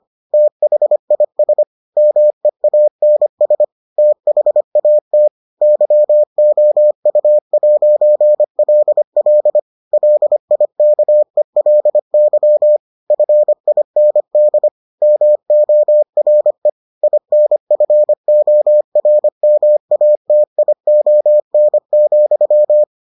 Morse code, heard as a keyed tone, about 25 wpm, 600 Hz. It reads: THIS MEANS THAT YOU'LL LIKELY FIND MORE INFORMATION,